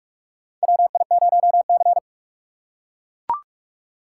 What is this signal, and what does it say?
Morse code, heard as a keyed tone, about 41 wpm, 700 Hz. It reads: WI0C
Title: USA Call Signs - 4 Characters - 3x Character Spacing - 45wpm